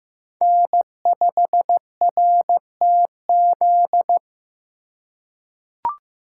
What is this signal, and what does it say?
Morse code, heard as a keyed tone, about 15 wpm, 700 Hz. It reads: N5RTZ
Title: USA Call Signs - 4x Character Spacing - 15wpm